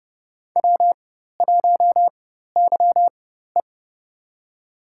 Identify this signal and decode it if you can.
Morse code, heard as a keyed tone, about 30 wpm, spaced as for 14 wpm, 700 Hz. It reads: W1YE
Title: USA Call Signs - 4 Characters - 4x Character Spacing - 30wpm